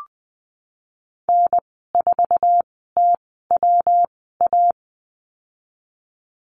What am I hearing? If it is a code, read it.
Morse code, heard as a keyed tone, about 20 wpm, spaced as for 15 wpm, 700 Hz. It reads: N4TWA
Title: 154 Lesson 25g - Letter p - Call Signs 20wpm